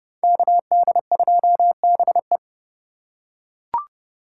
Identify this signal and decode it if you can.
Morse code, heard as a keyed tone, about 30 wpm, 700 Hz. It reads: KD2BE
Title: USA Call Signs - 5 Characters - 2x Character Spacing - 30wpm